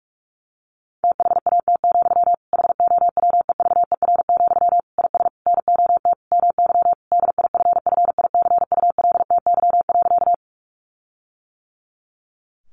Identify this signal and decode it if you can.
Morse code, heard as a keyed tone, about 44 wpm, 700 Hz. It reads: THAT, HOWEVER, IS NOT MY DIFFICULTY.
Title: Sentences from Top 1500 Words 45wpm